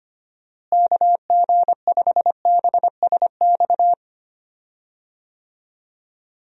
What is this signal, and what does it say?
Morse code, heard as a keyed tone, about 25 wpm, 700 Hz. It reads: KG5BSX
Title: USA Call Signs - 6 Characters - Repeated 3x - 25wpm